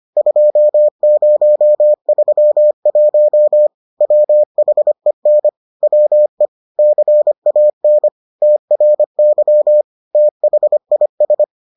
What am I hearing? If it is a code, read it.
Morse code, heard as a keyed tone, about 25 wpm, 600 Hz. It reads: 2031 WHEN WE CAN TRY THIS